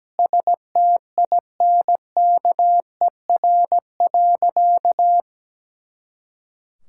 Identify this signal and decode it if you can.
Morse code, heard as a keyed tone, about 17 wpm, 700 Hz. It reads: STINKER.
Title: Pangram Challenge v2 - 17wpm